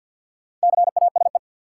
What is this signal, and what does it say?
Morse code, heard as a keyed tone, about 53 wpm, 700 Hz. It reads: CASE
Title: Four Letter Words - 50wpm